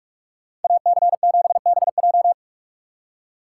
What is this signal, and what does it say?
Morse code, heard as a keyed tone, about 44 wpm, 700 Hz. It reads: AC7BJ